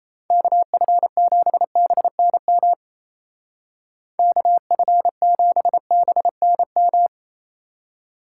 Morse code, heard as a keyed tone, about 33 wpm, 700 Hz. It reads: KF7BNM KF7BNM
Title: Speed Racing - USA 2x3 Call Signs 28wpm